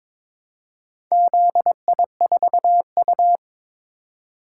Morse code, heard as a keyed tone, about 22 wpm, 700 Hz. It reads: ZI4U
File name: practice / 22wpm / Four Letter-Number Combinations - 22wpm